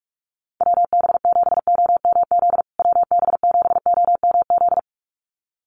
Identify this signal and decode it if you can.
Morse code, heard as a keyed tone, about 44 wpm, 700 Hz. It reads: WB7OMZ WB7OMZ
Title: USA Call Signs Repeated 2x 45wpm